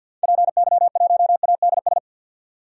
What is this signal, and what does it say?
Morse code, heard as a keyed tone, about 50 wpm, 700 Hz. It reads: PY1ADS
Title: DXCC-100 Call Signs - 6 Characters - 50wpm